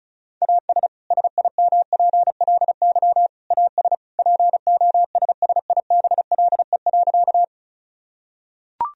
Morse code, heard as a keyed tone, about 36 wpm, 700 Hz. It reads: AS SIMPLY AS POSSIBLE.
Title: Sentences from Top 1000 Words - Review Part 4 - 35wpm